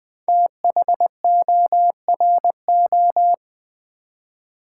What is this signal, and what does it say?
Morse code, heard as a keyed tone, about 20 wpm, 700 Hz. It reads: THORO
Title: Build A Word - 5 Letters At A Time - 10-Letter Words - 20wpm